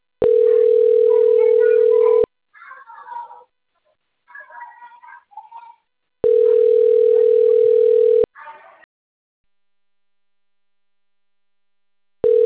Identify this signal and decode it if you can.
Ringback tone (North America), 440 + 480 Hz.
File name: rg-1600-unknown- - - .37214